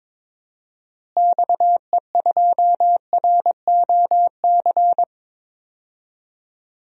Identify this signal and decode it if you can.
Morse code, heard as a keyed tone, about 22 wpm, 700 Hz. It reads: XE2ROC
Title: DXCC-100 Call Signs - 6 Characters - Repeated 2x - 22wpm